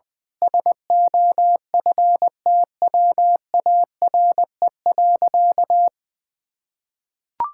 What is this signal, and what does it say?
Morse code, heard as a keyed tone, about 21 wpm, 700 Hz. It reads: SOFTWARE.
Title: Sentences with 6-Word Phrases - 20wpm